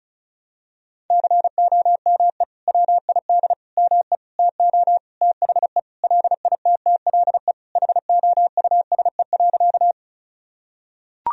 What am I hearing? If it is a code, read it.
Morse code, heard as a keyed tone, about 34 wpm, 700 Hz. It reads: COME WID ME TO THE LITTLE HOUSE.
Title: Sentences from Top 200 Words and Abbreviations 35wpm